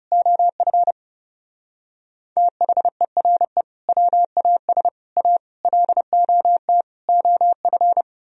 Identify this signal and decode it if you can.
Morse code, heard as a keyed tone, about 34 wpm, 700 Hz. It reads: OF THERE WAS A LOT OF
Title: Speed-Racing - Sets of 5 Words 30wpm